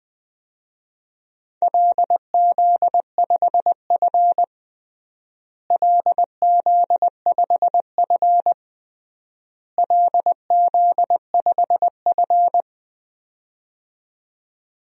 Morse code, heard as a keyed tone, about 20 wpm, 700 Hz. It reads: LZ5F LZ5F LZ5F
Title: DXCC-100 Call Signs - 4 Characters - Repeated 3x - 20wpm